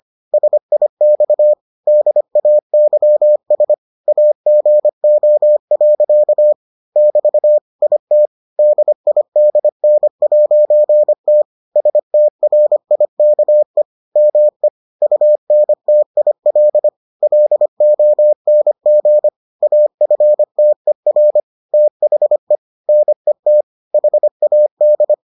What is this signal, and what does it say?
Morse code, heard as a keyed tone, about 25 wpm, 600 Hz. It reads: SIX DAYS AGO. <BT> IT DIDN'T STRIKE ME UNTIL LONG AFTER THE NET HAD